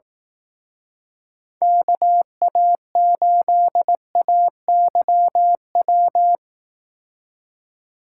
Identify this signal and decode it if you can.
Morse code, heard as a keyed tone, about 18 wpm, 700 Hz. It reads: KA8AYW
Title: Speed Racing - USA 2x3 Call Signs 15wpm